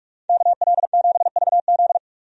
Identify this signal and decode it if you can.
Morse code, heard as a keyed tone, about 46 wpm, 700 Hz. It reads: KL7VZ
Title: DXCC-100 Call Signs - 5 Characters - 45wpm